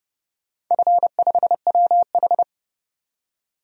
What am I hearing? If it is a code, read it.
Morse code, heard as a keyed tone, about 29 wpm, 700 Hz. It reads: F5WH
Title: DXCC-100 Call Signs - 4 Characters - 30wpm